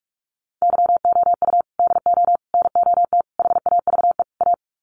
Morse code, heard as a keyed tone, about 44 wpm, 700 Hz. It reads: YOU DO NOT HAVE A
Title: Sets of 5 Words 45wpm